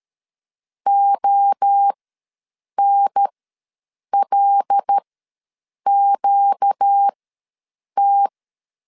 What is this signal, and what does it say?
Morse code, heard as a keyed tone, about 13 wpm, spaced as for 7 wpm, 800 Hz. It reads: ONLQT
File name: LEZIONI / 5 LBVC / 5 L ws 12wpm 2958